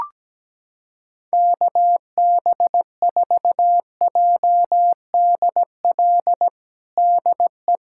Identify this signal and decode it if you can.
Morse code, heard as a keyed tone, about 17 wpm, 700 Hz. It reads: KB4JDL DE
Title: USA Callsign DE Callsign 17wpm